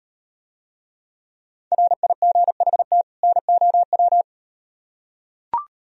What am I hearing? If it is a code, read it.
Morse code, heard as a keyed tone, about 38 wpm, 700 Hz. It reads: RIGHT NOW